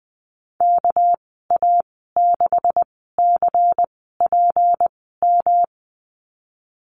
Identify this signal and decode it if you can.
Morse code, heard as a keyed tone, about 20 wpm, spaced as for 15 wpm, 700 Hz. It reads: KA6CPM
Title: 207 Lesson 33g - Letter j - Call Signs 20wpm